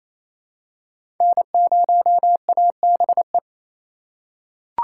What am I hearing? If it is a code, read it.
Morse code, heard as a keyed tone, about 27 wpm, 700 Hz. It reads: N0ABE
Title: USA Call Signs - 5 Characters - 4x Character Spacing - 28wpm